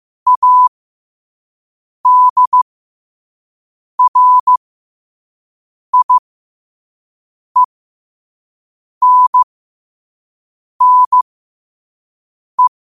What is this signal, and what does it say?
Morse code, heard as a keyed tone, about 15 wpm, spaced as for 5 wpm, 1000 Hz. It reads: ADRIENNE
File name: qotdfiles / 5wpm